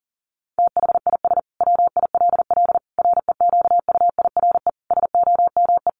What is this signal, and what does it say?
Morse code, heard as a keyed tone, about 40 wpm, 700 Hz. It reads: THIS WILL REQUIRE SOME
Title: Sentences from Top 900 Words 40wpm